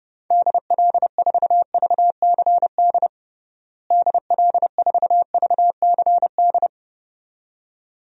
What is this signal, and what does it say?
Morse code, heard as a keyed tone, about 32 wpm, 700 Hz. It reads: DL4VCD DL4VCD
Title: DXCC-100 Call Signs - 6 Characters - Repeated 2x - 30wpm